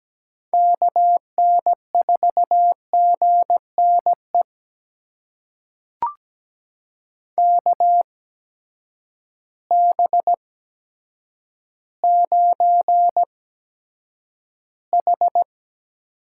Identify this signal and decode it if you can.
Morse code, heard as a keyed tone, about 17 wpm, spaced as for 8 wpm, 700 Hz. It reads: KN4GNE KB9H
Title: USA Call Signs - 6 Characters - 8x Character Spacing - 17wpm